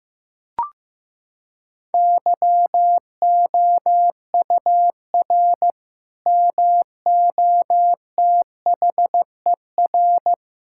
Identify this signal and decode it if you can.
Morse code, heard as a keyed tone, about 15 wpm, 700 Hz. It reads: YOUR MOTHER
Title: Sentences from Top 1600 Words 15wpm